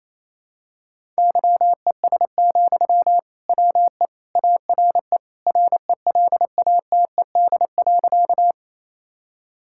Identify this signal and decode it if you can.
Morse code, heard as a keyed tone, about 27 wpm, 700 Hz. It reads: YES, WE ARE RELATED.
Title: Sentences from Top 1000 Words - Review Part 5 - 28wpm